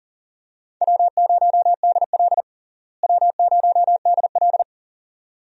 Morse code, heard as a keyed tone, about 40 wpm, 700 Hz. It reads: W0DL W0DL
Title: USA Call Signs - 4 Characters - Repeated 2x - 40wpm